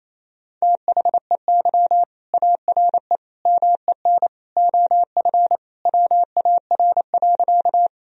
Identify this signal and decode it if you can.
Morse code, heard as a keyed tone, about 28 wpm, 700 Hz. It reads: THEY ARE MEN OF WAR.
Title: Sentences from Top 1000 Words - Review Part 2 - 28wpm